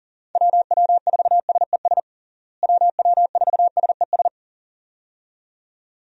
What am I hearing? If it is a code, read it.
Morse code, heard as a keyed tone, about 40 wpm, 700 Hz. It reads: WW4SES WW4SES
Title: USA Call Signs - 6 Characters - Repeated 2x - 40wpm